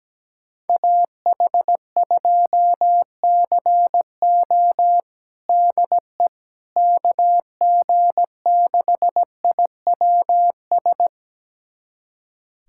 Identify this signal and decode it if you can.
Morse code, heard as a keyed tone, about 17 wpm, 700 Hz. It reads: AH2CO DE KG6IWS
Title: USA Callsign DE Callsign 17wpm